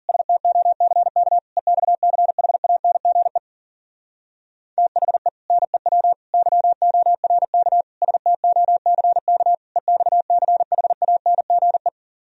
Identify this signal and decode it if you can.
Morse code, heard as a keyed tone, about 50 wpm, 700 Hz. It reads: STOCK EXCHANGE THE NEW YORK STOCK EXCHANGE